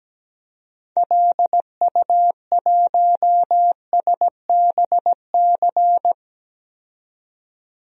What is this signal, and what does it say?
Morse code, heard as a keyed tone, about 17 wpm, 700 Hz. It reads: LU1SBC
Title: DXCC-100 Call Signs - 6 Characters - 17wpm